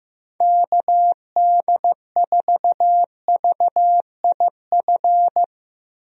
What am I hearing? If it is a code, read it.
Morse code, heard as a keyed tone, about 15 wpm, 700 Hz. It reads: KD4VIF
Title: USA Call Signs - 6 Characters - Repeated 3x - 15wpm